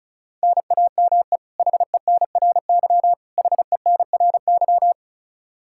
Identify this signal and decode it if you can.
Morse code, heard as a keyed tone, about 36 wpm, 700 Hz. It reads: NAME HENRY HENRY
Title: Practice QSO - Exchange 1 - Beginning - 35wpm